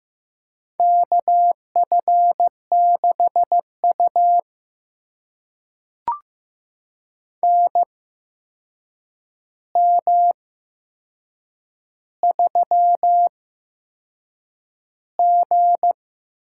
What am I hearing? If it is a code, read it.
Morse code, heard as a keyed tone, about 15 wpm, spaced as for 6 wpm, 700 Hz. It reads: KF6U NM3G
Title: USA Call Signs - 4 Characters - 8x Character Spacing - 15wpm